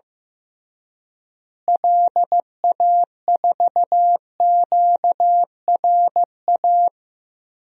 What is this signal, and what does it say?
Morse code, heard as a keyed tone, about 15 wpm, 700 Hz. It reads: LA4QRA
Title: DXCC-100 Call Signs - 6 Characters - Repeated 3x - 15wpm